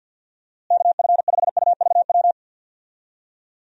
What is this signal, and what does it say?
Morse code, heard as a keyed tone, about 51 wpm, 700 Hz. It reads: KF5UVW
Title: USA Call Signs - 6 Characters - Repeated 2x - 50wpm